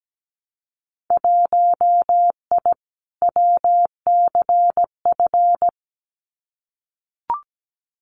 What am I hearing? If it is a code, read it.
Morse code, heard as a keyed tone, about 17 wpm, 700 Hz. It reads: 1I WCF
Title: Winter Field Day Exchanges Encore 17wpm